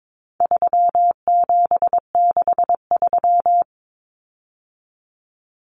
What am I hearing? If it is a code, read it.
Morse code, heard as a keyed tone, about 22 wpm, 700 Hz. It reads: 3763
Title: Common Cut Numbers - 4-Digit 22wpm